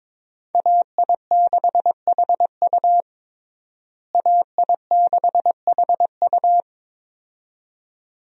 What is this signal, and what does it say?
Morse code, heard as a keyed tone, about 22 wpm, 700 Hz. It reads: AI6HU AI6HU
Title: USA Call Signs - 5 Characters - Repeated 2x - 22wpm